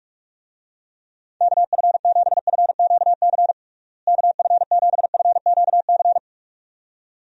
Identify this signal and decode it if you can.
Morse code, heard as a keyed tone, about 41 wpm, 700 Hz. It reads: KF7FQC KF7FQC
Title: USA Call Signs - 6 Characters - Repeated 2x - 45wpm